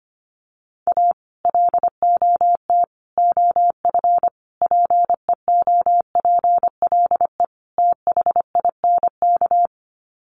Morse code, heard as a keyed tone, about 25 wpm, 700 Hz. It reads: A LOT OF PEOPLE THINK
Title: Sets of 5 Words Encore 25wpm